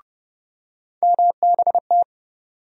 Morse code, heard as a keyed tone, about 31 wpm, 700 Hz. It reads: MBT
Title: USA Call Sign Postfixes - 3 Letters 30wpm